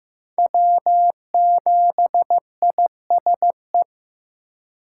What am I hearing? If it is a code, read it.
Morse code, heard as a keyed tone, about 18 wpm, 700 Hz. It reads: W7ISE
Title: Speed Racing - USA 1x3 Call Signs 15wpm